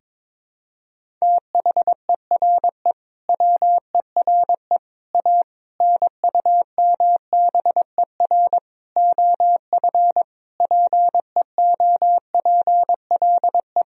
Morse code, heard as a keyed tone, about 22 wpm, 700 Hz. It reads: THERE WERE A NUMBER OF PEOPLE
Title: Sentences from Top 1000 Words - Review Part 1 - 22wpm